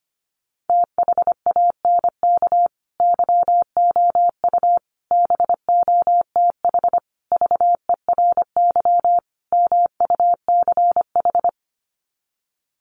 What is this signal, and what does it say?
Morse code, heard as a keyed tone, about 25 wpm, 700 Hz. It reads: THANK YOU BOTH VERY MUCH
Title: Sets of 5 Words 25wpm